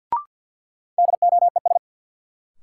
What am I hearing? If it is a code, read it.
Morse code, heard as a keyed tone, about 52 wpm, 700 Hz. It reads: DOES
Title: Sentences from Top 200 Words - New Word Drill - 50wpm